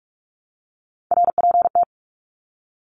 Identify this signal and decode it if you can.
Morse code, heard as a keyed tone, about 44 wpm, 700 Hz. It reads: RPT
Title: Abbreviations Repeated 3x - 45wpm